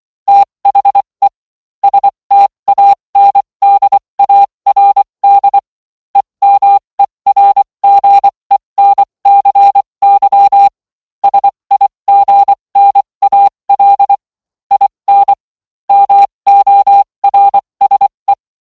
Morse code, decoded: THE STANDARD EMERGENCY SIGNAL IN MORSE